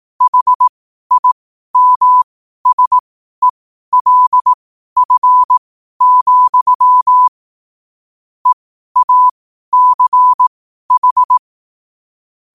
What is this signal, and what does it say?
Morse code, heard as a keyed tone, about 18 wpm, spaced as for 12 wpm, 1000 Hz. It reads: HIMSELF, EACH
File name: qotdfiles / 12wpm